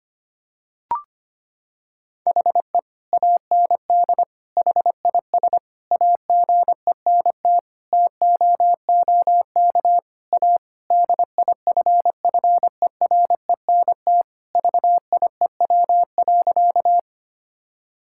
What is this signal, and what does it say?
Morse code, heard as a keyed tone, about 25 wpm, 700 Hz. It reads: HE AND HIS AGENT TOOK A DIFFERENT VIEW.
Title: Sentences from Top 1600 Words 25wpm